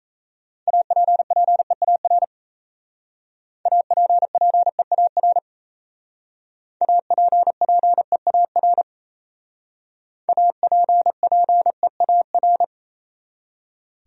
Morse code, heard as a keyed tone, about 34 wpm, 700 Hz. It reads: APPEAR APPEAR APPEAR APPEAR
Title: Speed-Racing Top 500 Words 28wpm